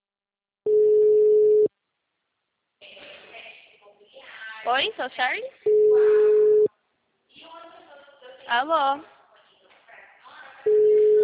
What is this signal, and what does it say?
Ringback tone (Europe, 425 Hz), with speech.